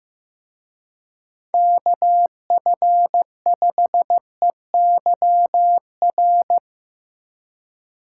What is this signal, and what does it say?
Morse code, heard as a keyed tone, about 17 wpm, 700 Hz. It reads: KF5EYR
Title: Speed Racing - USA 2x3 Call Signs 15wpm